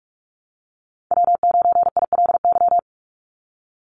Morse code, heard as a keyed tone, about 46 wpm, 700 Hz. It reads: W9ILY
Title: Call Signs - 45wpm